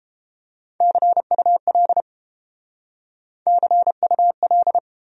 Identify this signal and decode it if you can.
Morse code, heard as a keyed tone, about 31 wpm, 700 Hz. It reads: CUL CUL